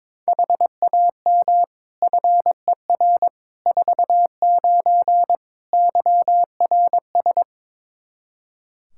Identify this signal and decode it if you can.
Morse code, heard as a keyed tone, about 22 wpm, 700 Hz. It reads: HAM FER 49 YRS
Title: K1USN SST QSO Practice 22wpm